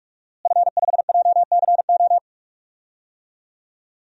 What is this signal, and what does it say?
Morse code, heard as a keyed tone, about 46 wpm, 700 Hz. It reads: F5JCO
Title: DXCC-100 Call Signs - 5 Characters - Repeated 3x - 45wpm